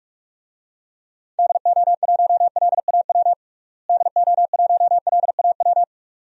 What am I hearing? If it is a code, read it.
Morse code, heard as a keyed tone, about 46 wpm, 700 Hz. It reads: DO1LAW DO1LAW
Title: DXCC-100 Call Signs - 6 Characters - Repeated 2x - 45wpm